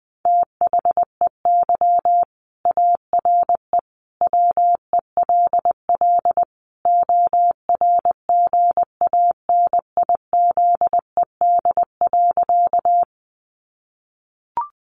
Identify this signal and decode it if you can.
Morse code, heard as a keyed tone, about 20 wpm, 700 Hz. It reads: THEY ARE WELL ORGANIZED.
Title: Sentences from Top 1300 Words 20wpm